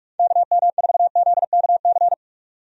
Morse code, heard as a keyed tone, about 46 wpm, 700 Hz. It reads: KM4ZKC
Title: USA Call Signs - 6 Characters - 45wpm